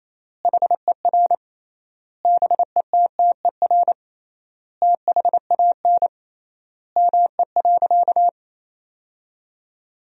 Morse code, heard as a keyed tone, about 28 wpm, 700 Hz. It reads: HER BETTER THAN ME.
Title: 3x Word-Spacing - Sentences from Top 400 Words - 28wpm